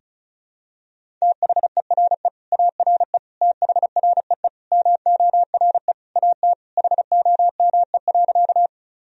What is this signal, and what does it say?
Morse code, heard as a keyed tone, about 36 wpm, 700 Hz. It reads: THERE ARE THREE MORE AT HOME.
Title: Sentences from Top 1000 Words - Review Part 1 - 35wpm